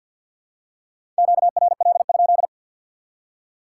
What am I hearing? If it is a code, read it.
Morse code, heard as a keyed tone, about 54 wpm, 700 Hz. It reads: QRL?
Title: Top 180 QSO Elements 50wpm